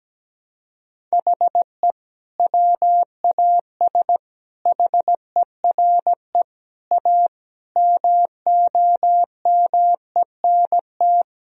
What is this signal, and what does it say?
Morse code, heard as a keyed tone, about 17 wpm, 700 Hz. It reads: HE WAS HERE A MOMENT
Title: Sentences from Top 1000 Words - Review Part 3 - 17wpm